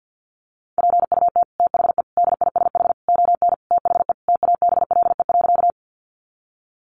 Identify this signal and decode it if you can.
Morse code, heard as a keyed tone, about 50 wpm, 700 Hz. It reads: PUT THE DISH ON THE TABLE.
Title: Sentences from Top 1300 Words 50wpm